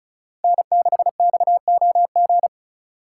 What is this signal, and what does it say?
Morse code, heard as a keyed tone, about 45 wpm, 700 Hz. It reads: N6XOG